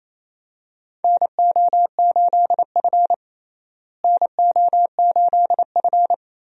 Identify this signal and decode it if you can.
Morse code, heard as a keyed tone, about 28 wpm, 700 Hz. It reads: NO8F NO8F
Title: USA Call Signs - 4 Characters - Repeated 3x - 28wpm